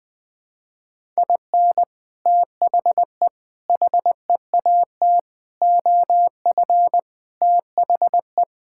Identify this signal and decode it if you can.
Morse code, heard as a keyed tone, about 20 wpm, 700 Hz. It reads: IN THE HEAT OF THE
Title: Sets of 6 Words - 20wpm